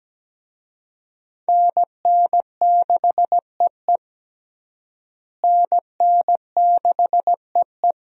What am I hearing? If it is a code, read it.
Morse code, heard as a keyed tone, about 17 wpm, 700 Hz. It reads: NN6EE NN6EE
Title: USA Call Signs - 5 Characters - Repeated 2x - 17wpm